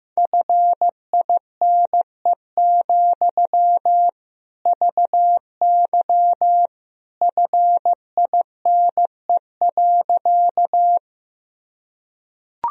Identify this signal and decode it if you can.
Morse code, heard as a keyed tone, about 15 wpm, 700 Hz. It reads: FINE, VY FINE.
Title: Sentences from Top 500 Words and Abbreviations 15wpm